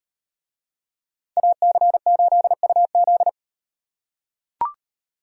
Morse code, heard as a keyed tone, about 38 wpm, 700 Hz. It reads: AC8UZ